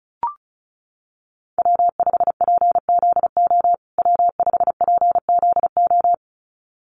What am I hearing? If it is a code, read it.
Morse code, heard as a keyed tone, about 34 wpm, 700 Hz. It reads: W5PZO W5PZO
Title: USA Call Signs Repeated 2x 35wpm